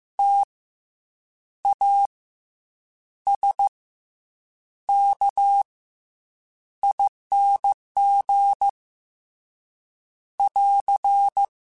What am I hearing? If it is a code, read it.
Morse code, heard as a keyed tone, about 15 wpm, 800 Hz. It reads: T A S K ING <AR>